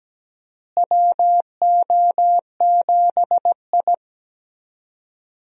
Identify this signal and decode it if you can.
Morse code, heard as a keyed tone, about 17 wpm, 700 Hz. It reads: WO7I
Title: USA Call Signs - 4 Characters - Repeated 3x - 17wpm